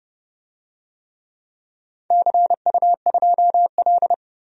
Morse code, heard as a keyed tone, about 30 wpm, 700 Hz. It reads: CU2L
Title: DXCC-100 Call Signs - 4 Characters - 30wpm